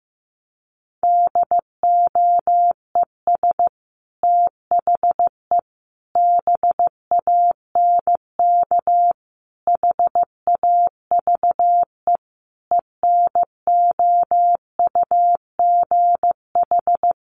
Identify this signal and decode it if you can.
Morse code, heard as a keyed tone, about 15 wpm, 700 Hz. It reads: DOES THE BANK HAVE ENOUGH
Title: Sentences from Top 1500 Words 15wpm